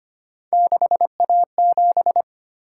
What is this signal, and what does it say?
Morse code, heard as a keyed tone, about 25 wpm, 700 Hz. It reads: 6A7
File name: Three Letter-Number Combinations - 25wpm